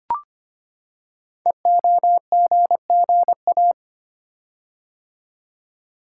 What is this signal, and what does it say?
Morse code, heard as a keyed tone, about 25 wpm, 700 Hz. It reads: EOGGA
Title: Code Groups - 5 Letters - 25wpm